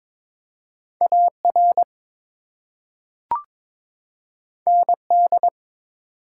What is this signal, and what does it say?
Morse code, heard as a keyed tone, about 21 wpm, 700 Hz. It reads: AR ND
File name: ARRL Sections v2 - 22wpm